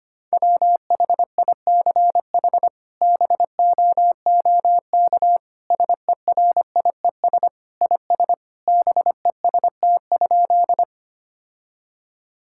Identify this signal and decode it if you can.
Morse code, heard as a keyed tone, about 25 wpm, 700 Hz. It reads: WHICH BOOK SERIES IS BEST?
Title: Sentences from Top 1000 Words - Review Part 5 - 25wpm